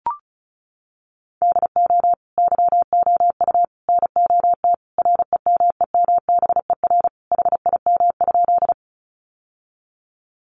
Morse code, heard as a keyed tone, about 34 wpm, 700 Hz. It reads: DO YOU NOT REMEMBER HIM?
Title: Sentences from Top 600 Words 35wpm